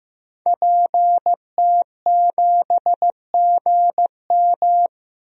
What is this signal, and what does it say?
Morse code, heard as a keyed tone, about 15 wpm, 700 Hz. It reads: PT7GM
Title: DXCC-100 Call Signs - 5 Characters - 15wpm